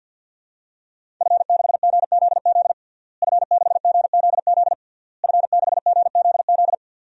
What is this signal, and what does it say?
Morse code, heard as a keyed tone, about 46 wpm, 700 Hz. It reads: F6GZZ F6GZZ F6GZZ
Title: DXCC-100 Call Signs - 5 Characters - Repeated 3x - 50wpm